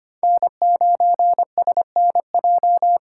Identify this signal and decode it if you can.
Morse code, heard as a keyed tone, about 25 wpm, 700 Hz. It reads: N9SNJ
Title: USA Call Signs - 5 Characters - 3x Character Spacing - 25wpm